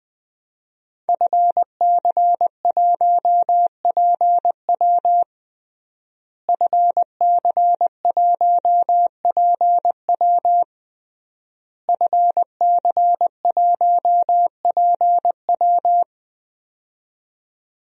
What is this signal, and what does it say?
Morse code, heard as a keyed tone, about 20 wpm, 700 Hz. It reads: FC1PW FC1PW FC1PW
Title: DXCC-100 Call Signs - 5 Characters - Repeated 3x - 20wpm